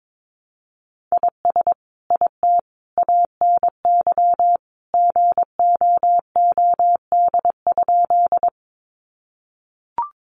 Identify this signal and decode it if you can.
Morse code, heard as a keyed tone, about 22 wpm, 700 Hz. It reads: IS IT ANY GOOD?
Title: Sentences from Top 200 Words 22wpm